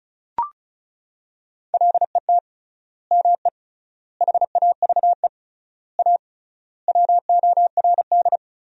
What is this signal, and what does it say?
Morse code, heard as a keyed tone, about 36 wpm, 700 Hz. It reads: LET ME HAVE A WORD
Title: 3x Word-Spacing - Sentences from Top 300 Words - 35wpm